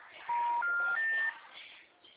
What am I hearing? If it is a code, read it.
Special information tone, with noise.
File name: out- -2009- - - .44391